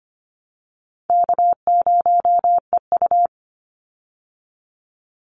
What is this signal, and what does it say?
Morse code, heard as a keyed tone, about 25 wpm, 700 Hz. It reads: K0EU
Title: Call Signs Repeated 2x - 25wpm